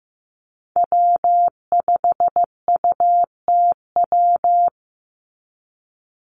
Morse code, heard as a keyed tone, about 15 wpm, 700 Hz. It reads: W5UTW
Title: Call Signs Repeated 3x - 15wpm